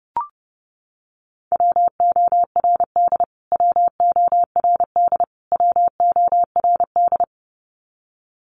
Morse code, heard as a keyed tone, about 30 wpm, 700 Hz. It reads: WORD WORD WORD
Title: Top 500 Words Repeated 30wpm